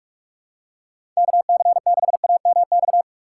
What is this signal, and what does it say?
Morse code, heard as a keyed tone, about 46 wpm, 700 Hz. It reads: KC6AMX
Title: USA Call Signs - 6x Character Spacing - 45wpm